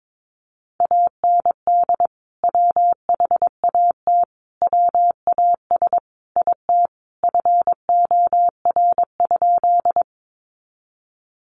Morse code, heard as a keyed tone, about 22 wpm, 700 Hz. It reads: AND WHAT WAS IT FOR?
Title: Sentences from Top 100 Words 22wpm